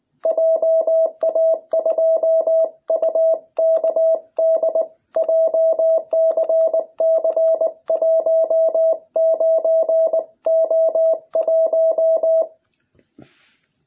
Morse code, heard as a keyed tone, about 20 wpm, 660 Hz. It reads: JA2UKD JCC19O1